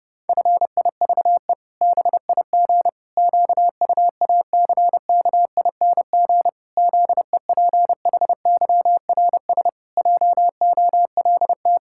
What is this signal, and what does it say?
Morse code, heard as a keyed tone, about 30 wpm, 700 Hz. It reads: FIVE BIG QUACKING ZEPHYRS JOLT
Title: Pangram Challenge v2 - 30wpm